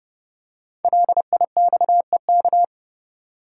Morse code, heard as a keyed tone, about 30 wpm, 700 Hz. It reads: LIXEK
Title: Code Groups - 5 Letter:Numbers - 30wpm